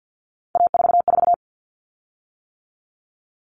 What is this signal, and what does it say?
Morse code, heard as a keyed tone, about 50 wpm, 700 Hz. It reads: A44
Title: Common Cut Numbers - 3-Digit 50wpm